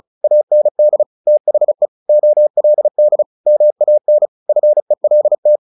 Morse code, heard as a keyed tone, about 34 wpm, 600 Hz. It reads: AND THE OLD MAN FELT
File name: the old man and the sea / the old man and the sea 1 35wpm